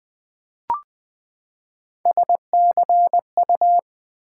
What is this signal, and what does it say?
Morse code, heard as a keyed tone, about 21 wpm, 700 Hz. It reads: SCU
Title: Build A Word - 3 Letters At A Time - 9-Letter Words - 20wpm